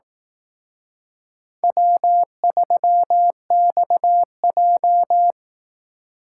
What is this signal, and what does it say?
Morse code, heard as a keyed tone, about 18 wpm, 700 Hz. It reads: W3XJ
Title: Speed Racing - USA 1x2 Call Signs 15wpm